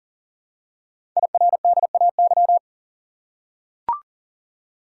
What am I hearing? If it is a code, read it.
Morse code, heard as a keyed tone, about 38 wpm, 700 Hz. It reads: IRDAY
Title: Code Groups - 5 Letters - 40wpm